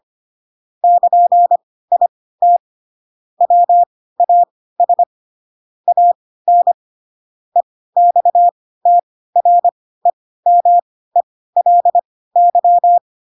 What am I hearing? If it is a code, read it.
Morse code, heard as a keyed tone, about 25 wpm, spaced as for 16 wpm, 700 Hz. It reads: <KN>IT WAS AN EXTREMELY
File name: Pinocchio / Pinocchio ch3 10 25 ik1hge